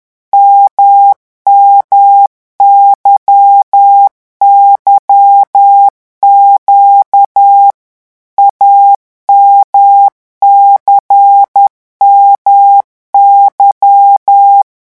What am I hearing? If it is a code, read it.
Morse code, heard as a keyed tone, about 10 wpm, 800 Hz. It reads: MMYYQ AMCMY
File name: morse mp3 / revision2-vitesse 10 mots